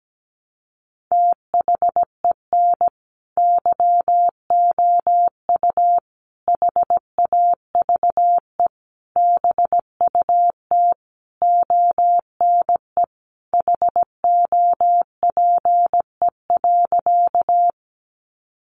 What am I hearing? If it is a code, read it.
Morse code, heard as a keyed tone, about 17 wpm, 700 Hz. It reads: THEN YOU HAVE BUT ONE HOPE.
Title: Sentences from Top 1000 Words 17wpm